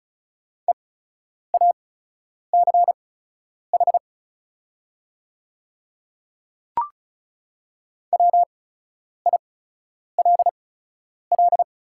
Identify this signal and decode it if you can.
Morse code, heard as a keyed tone, about 36 wpm, spaced as for 10 wpm, 700 Hz. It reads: EACH WILL
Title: 8x Character Spacing - Top 100 Words - 35wpm